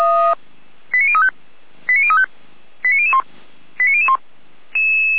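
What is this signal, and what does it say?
Five-tone selective call (ZVEI), heard as quick runs of five tones, tones of about 80 ms. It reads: 89035 89035 89001 89001
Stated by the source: ZVEI five-tone call received on SDR.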